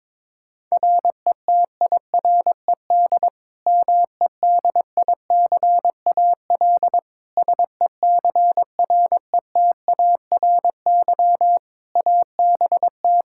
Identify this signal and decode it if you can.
Morse code, heard as a keyed tone, about 22 wpm, 700 Hz. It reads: RETIRED MEDICAL SECRETARY ABT
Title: K1USN SST QSO Practice 22wpm